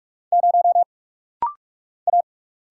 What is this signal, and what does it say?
Morse code, heard as a keyed tone, about 46 wpm, 700 Hz. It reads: 0 A
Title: Single Letter-Number Alt - Rapid-Fire 45wpm